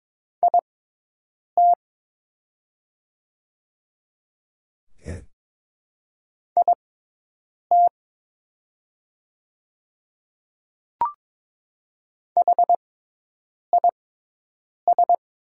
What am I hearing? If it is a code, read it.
Morse code, heard as a keyed tone, about 22 wpm, spaced as for 8 wpm, 700 Hz. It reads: IT IT HIS
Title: 6x Character Spacing - Top 100 Words - 22wpm